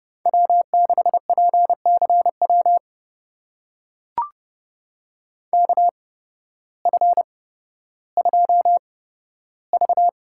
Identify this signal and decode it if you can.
Morse code, heard as a keyed tone, about 30 wpm, spaced as for 14 wpm, 700 Hz. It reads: W6PCW KF2V
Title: USA Call Signs - 5 Characters - 8x Character Spacing - 30wpm